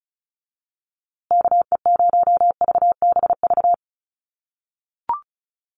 Morse code, heard as a keyed tone, about 34 wpm, 700 Hz. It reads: KE0VBV
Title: USA Call Signs Repeated 3x 35wpm